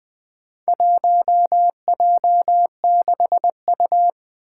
Morse code, heard as a keyed tone, about 20 wpm, 700 Hz. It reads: 1J6U
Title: Four Challenging Letter-Number Combinations 20wpm